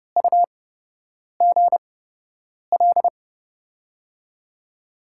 Morse code, heard as a keyed tone, about 30 wpm, spaced as for 8 wpm, 700 Hz. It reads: UGL
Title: USA Call Signs - 6 Characters - 8x Character Spacing - 30wpm